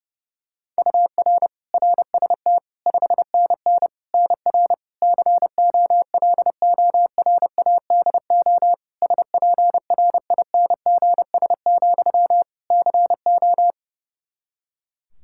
Morse code, heard as a keyed tone, about 32 wpm, 700 Hz. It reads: UR RST 5NN NR COLORADO SPRINGS, CO
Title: Practice QSO - Exchange 1 - Beginning - 30wpm